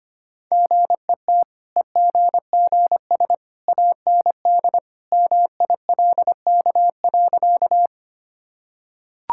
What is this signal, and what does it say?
Morse code, heard as a keyed tone, about 25 wpm, 690 Hz. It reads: GET EGGS AND MILK.
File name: Easy Sentences with Hints - 25wpm